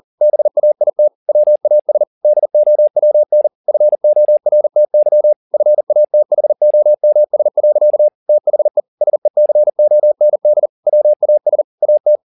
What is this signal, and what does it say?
Morse code, heard as a keyed tone, about 40 wpm, 600 Hz. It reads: BAIT WAS DOWN FORTY FATHOMS. THE SECOND WAS AT